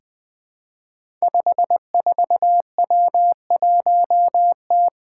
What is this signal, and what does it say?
Morse code, heard as a keyed tone, about 20 wpm, 700 Hz. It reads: 54W1T
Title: Five Challenging Letter-Number Combinations 20wpm